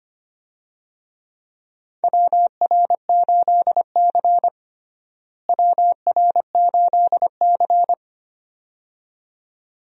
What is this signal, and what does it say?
Morse code, heard as a keyed tone, about 25 wpm, 700 Hz. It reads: WR8C WR8C
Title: USA Call Signs - 4 Characters - Repeated 2x - 25wpm